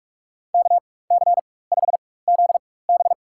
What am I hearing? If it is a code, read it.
Morse code, heard as a keyed tone, about 46 wpm, spaced as for 21 wpm, 690 Hz. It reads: KC5ZB
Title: USA Call Signs - 4x Character Spacing - 45wpm